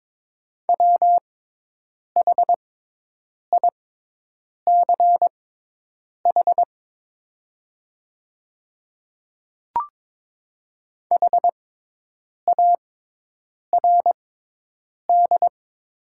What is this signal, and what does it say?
Morse code, heard as a keyed tone, about 22 wpm, spaced as for 8 wpm, 700 Hz. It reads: WHICH HARD
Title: 6x Character Spacing - Top 300 Words - 22wpm